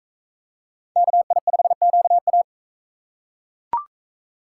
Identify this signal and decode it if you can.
Morse code, heard as a keyed tone, about 36 wpm, 700 Hz. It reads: KI5QA